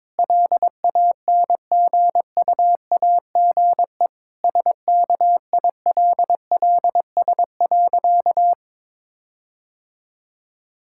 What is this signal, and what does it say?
Morse code, heard as a keyed tone, about 22 wpm, 700 Hz. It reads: LANGUAGE SKILLS.
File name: Difficult Sentences with Hints - 22wpm